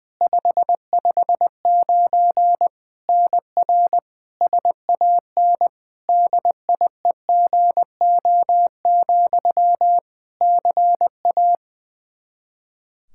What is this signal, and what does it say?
Morse code, heard as a keyed tone, about 20 wpm, 700 Hz. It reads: 559 NR SAN DIEGO, CA
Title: Practice QSO - Exchange 1 - Beginning - 20wpm